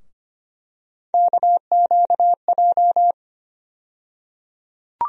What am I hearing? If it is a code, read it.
Morse code, heard as a keyed tone, about 25 wpm, 700 Hz. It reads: KQJ
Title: USA Call Sign Postfixes - 3 Letters 25wpm